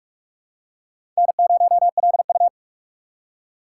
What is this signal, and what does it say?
Morse code, heard as a keyed tone, about 44 wpm, 700 Hz. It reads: N0LU